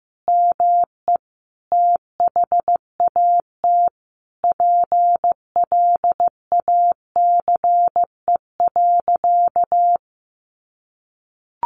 Morse code, heard as a keyed tone, about 15 wpm, 700 Hz. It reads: ME THAT PLACE.
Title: Sentences from Top 800 Words 15wpm